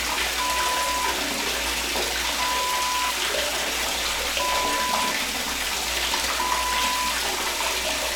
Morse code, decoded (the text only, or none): none